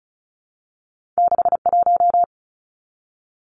Morse code, heard as a keyed tone, about 36 wpm, 700 Hz. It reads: 61
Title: Numbers - 2-Digit 35wpm